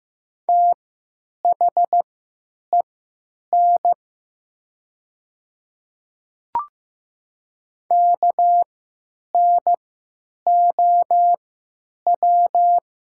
Morse code, heard as a keyed tone, about 15 wpm, spaced as for 9 wpm, 690 Hz. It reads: THEN KNOW
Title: 3x Character Spacing - Top 100 Words - 15wpm